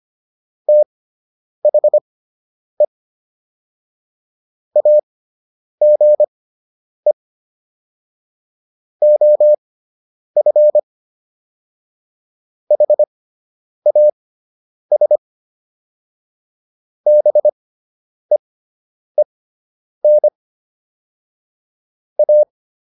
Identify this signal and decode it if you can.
Morse code, heard as a keyed tone, about 25 wpm, spaced as for 9 wpm, 600 Hz. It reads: THE AGE OF HAS BEEN A